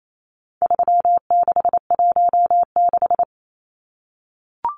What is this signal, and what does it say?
Morse code, heard as a keyed tone, about 28 wpm, 700 Hz. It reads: 3616
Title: Numbers - 4-Digit 28wpm